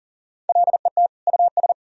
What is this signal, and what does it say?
Morse code, heard as a keyed tone, about 40 wpm, 700 Hz. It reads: LET US
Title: Sentences from Top 400 Words and Abbreviations 40wpm